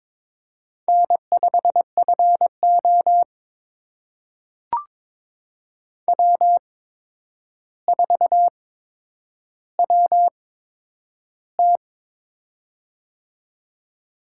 Morse code, heard as a keyed tone, about 22 wpm, spaced as for 9 wpm, 700 Hz. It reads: N5FO W4WT
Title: USA Call Signs - 4 Characters - 8x Character Spacing - 22wpm